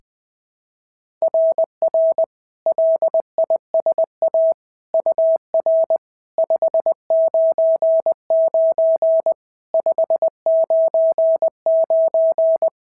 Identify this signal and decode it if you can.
Morse code, heard as a keyed tone, about 20 wpm, 690 Hz. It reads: RR LISA UR 599 599
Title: SOTA QSO Practice 20wpm